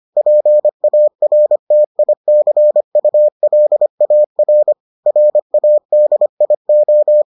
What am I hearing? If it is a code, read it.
Morse code, heard as a keyed tone, about 25 wpm, 600 Hz. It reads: PARTICULAR RADIO